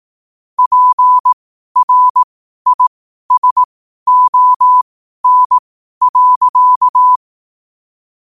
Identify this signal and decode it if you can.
Morse code, heard as a keyed tone, about 18 wpm, spaced as for 12 wpm, 1000 Hz. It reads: PRISON.
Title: 12 WPM morse code quotes for Thu, 29 Feb 2024